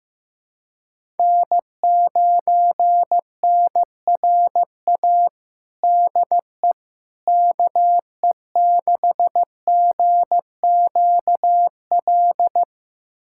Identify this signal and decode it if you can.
Morse code, heard as a keyed tone, about 15 wpm, 700 Hz. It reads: N9NRA DE KE6GQL
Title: USA Callsign DE Callsign 15wpm